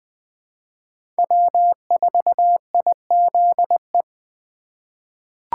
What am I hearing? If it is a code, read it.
Morse code, heard as a keyed tone, about 21 wpm, 700 Hz. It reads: W4IZE
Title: USA Call Signs - 5 Characters - 3x Character Spacing - 20wpm